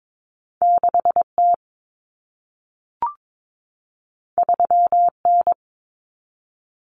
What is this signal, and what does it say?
Morse code, heard as a keyed tone, about 22 wpm, 700 Hz. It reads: 6T 3N
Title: Common Cut Numbers - 2-Digit 22wpm